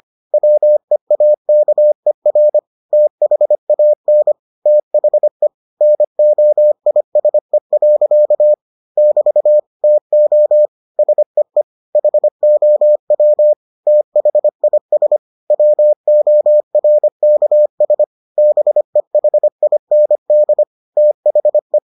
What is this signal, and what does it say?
Morse code, heard as a keyed tone, about 25 wpm, 600 Hz. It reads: WEAKER THAN THE NOISE. <BT> TO SEE HOW THIS WORKS BEHIND THE